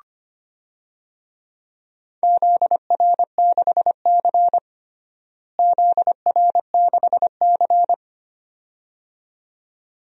Morse code, heard as a keyed tone, about 25 wpm, 700 Hz. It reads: ZR6C ZR6C
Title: DXCC-100 Call Signs - 4 Characters - Repeated 2x - 25wpm